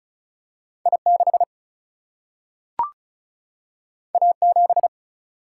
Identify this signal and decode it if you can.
Morse code, heard as a keyed tone, about 36 wpm, 700 Hz. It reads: I6 A7
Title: Single-Letter Single-Number 35wpm